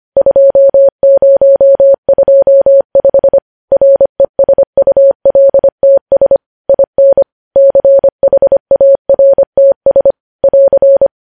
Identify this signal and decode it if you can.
Morse code, heard as a keyed tone, about 25 wpm, 550 Hz. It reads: 2025 RESULTS IN CHARTS <AR>
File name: podcast / News headlines-1x-25x25